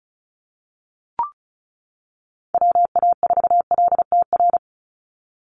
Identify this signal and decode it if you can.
Morse code, heard as a keyed tone, about 34 wpm, 700 Hz. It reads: WA4LTR
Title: USA Call Signs 35wpm